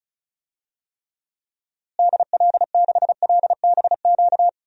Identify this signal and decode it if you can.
Morse code, heard as a keyed tone, about 36 wpm, 700 Hz. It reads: DL6LBQ
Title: DXCC-100 Call Signs - 6 Characters - Repeated 3x - 35wpm